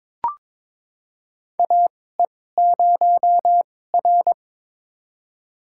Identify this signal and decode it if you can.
Morse code, heard as a keyed tone, about 22 wpm, spaced as for 16 wpm, 700 Hz. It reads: AE0R
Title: USA Call Signs - 4 Characters - 2x Character Spacing - 22wpm